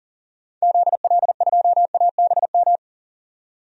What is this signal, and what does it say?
Morse code, heard as a keyed tone, about 40 wpm, 700 Hz. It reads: ZL2ABM
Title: DXCC-100 Call Signs - 6 Characters - 40wpm